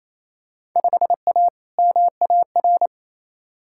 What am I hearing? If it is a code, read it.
Morse code, heard as a keyed tone, about 27 wpm, 700 Hz. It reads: 5A MAR
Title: Field Day Exchanges Encore - 28wpm